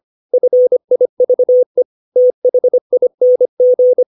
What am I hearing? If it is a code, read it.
Morse code, heard as a keyed tone, about 25 wpm, 500 Hz. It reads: FIVE THING
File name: 632 words / 632 Words 25WPM